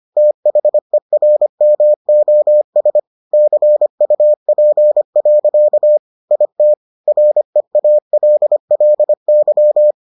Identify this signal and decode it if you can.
Morse code, heard as a keyed tone, about 25 wpm, 600 Hz. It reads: THERMOS CUP. IT REALLY